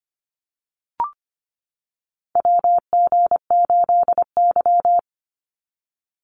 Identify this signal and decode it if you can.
Morse code, heard as a keyed tone, about 25 wpm, 700 Hz. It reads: WG8Y
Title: NA SOTA Chaser Callsigns 25wpm